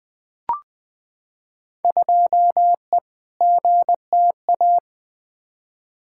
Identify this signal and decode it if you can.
Morse code, heard as a keyed tone, about 20 wpm, 700 Hz. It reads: 2E GTA
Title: Field Day Exchanges Encore - 20wpm